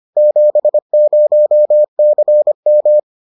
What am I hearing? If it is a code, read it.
Morse code, heard as a keyed tone, about 25 wpm, 600 Hz. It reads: 70CM